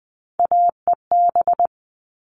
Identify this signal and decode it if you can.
Morse code, heard as a keyed tone, about 20 wpm, 700 Hz. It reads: AEB
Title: All Cut Numbers - 3-Digit 20wpm